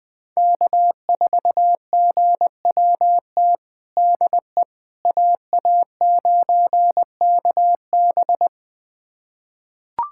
Morse code, heard as a keyed tone, about 21 wpm, 700 Hz. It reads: K4GWT DE AA9KB
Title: USA Callsign DE Callsign 20wpm